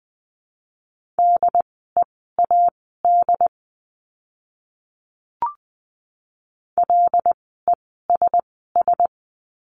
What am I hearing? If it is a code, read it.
Morse code, heard as a keyed tone, about 20 wpm, spaced as for 15 wpm, 700 Hz. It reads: DEAD LESS
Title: 075 Lesson 13f - Letter L - Words 20wpm